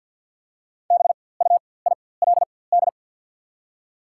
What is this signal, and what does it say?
Morse code, heard as a keyed tone, about 52 wpm, spaced as for 24 wpm, 690 Hz. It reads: BUILD
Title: 4x Character Spacing - Top 500 Words - 50wpm